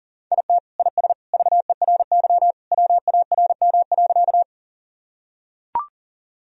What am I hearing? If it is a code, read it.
Morse code, heard as a keyed tone, about 40 wpm, 700 Hz. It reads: IT IS VERY WARM.
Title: Sentences from Top 1000 Words - Review Part 2 - 40wpm